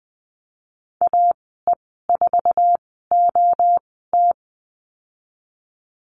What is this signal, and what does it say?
Morse code, heard as a keyed tone, about 20 wpm, spaced as for 15 wpm, 700 Hz. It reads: AE4OT
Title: 055 Lesson 10g - Letter r - Call Signs 20wpm